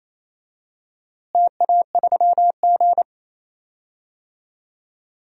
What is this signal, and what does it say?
Morse code, heard as a keyed tone, about 28 wpm, 700 Hz. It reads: TA3G
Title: DXCC-100 Call Signs - 4 Characters - Repeated 3x - 28wpm